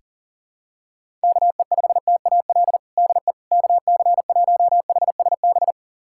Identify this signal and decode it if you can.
Morse code, heard as a keyed tone, about 40 wpm, 700 Hz. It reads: KE5TAL DE KC1HSB
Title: USA Callsign DE Callsign - 6 Characters - 40wpm